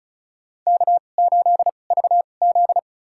Morse code, heard as a keyed tone, about 34 wpm, spaced as for 25 wpm, 690 Hz. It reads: K8VZ
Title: USA Call Signs - 4 Characters - 2x Character Spacing - 35wpm